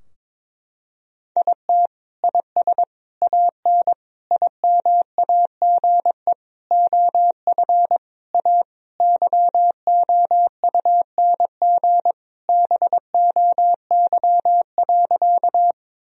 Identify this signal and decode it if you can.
Morse code, heard as a keyed tone, about 22 wpm, 700 Hz. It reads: IT IS AN IMAGE OF A YOUNG BOY.
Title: Sentences from Top 1000 Words - Review Part 4 - 22wpm